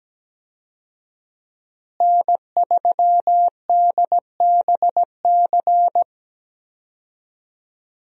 Morse code, heard as a keyed tone, about 17 wpm, 700 Hz. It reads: N3DBC
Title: USA Call Signs - 5 Characters - Repeated 2x - 17wpm